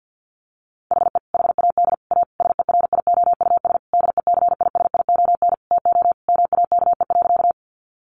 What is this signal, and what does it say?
Morse code, heard as a keyed tone, about 50 wpm, 700 Hz. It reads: HE HAD A SERIOUS DECISION TO MAKE.
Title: Sentences from Top 1100 Words 50wpm